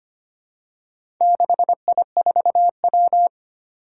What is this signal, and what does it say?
Morse code, heard as a keyed tone, about 25 wpm, 700 Hz. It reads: 6I4W
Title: Four Challenging Letter-Number Combinations 25wpm